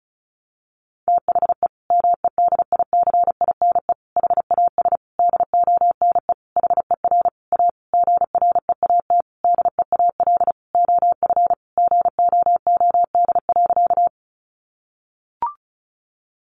Morse code, heard as a keyed tone, about 34 wpm, 700 Hz. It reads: THE MEDICINE HAS DONE HER A GREAT DEAL OF GOOD.
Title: Sentences from Top 900 Words 35wpm